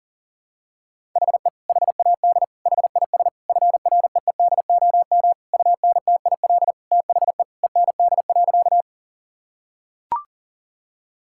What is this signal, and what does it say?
Morse code, heard as a keyed tone, about 40 wpm, 700 Hz. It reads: HE HAD HIS FREEDOM UNTIL THE END.
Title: Sentences from Top 1000 Words - Review Part 5 - 40wpm